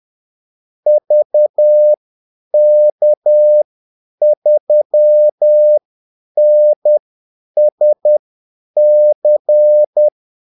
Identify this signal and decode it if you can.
Morse code, heard as a keyed tone, about 10 wpm, spaced as for 8 wpm, 600 Hz. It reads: VK3NSC
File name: Callsigns-10-8wpm